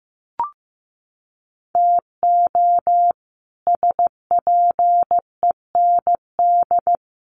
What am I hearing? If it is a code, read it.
Morse code, heard as a keyed tone, about 15 wpm, 700 Hz. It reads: TO SPEND
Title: Sets of 5 Words Encore2 15wpm